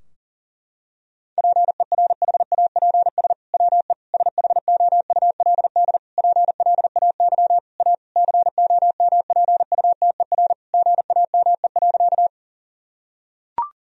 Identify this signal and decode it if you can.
Morse code, heard as a keyed tone, about 40 wpm, 700 Hz. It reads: PERHAPS WE SHOULD PLAY A COMPUTER GAME.
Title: Sentences from Top 1000 Words - Review Part 3 - 40wpm